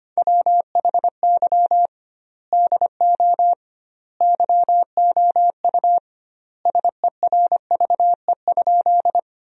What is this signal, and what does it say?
Morse code, heard as a keyed tone, about 25 wpm, 700 Hz. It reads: WHY DO YOU SERVE?
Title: 2x Word-Spacing - Sentences from Top 400 Words - 25wpm